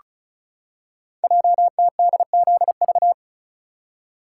Morse code, heard as a keyed tone, about 36 wpm, 700 Hz. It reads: JTDZV
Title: Code Groups - 5 Letters - 35wpm